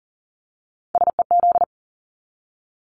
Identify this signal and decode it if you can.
Morse code, heard as a keyed tone, about 40 wpm, 700 Hz. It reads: SEZ
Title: Abbreviations 40wpm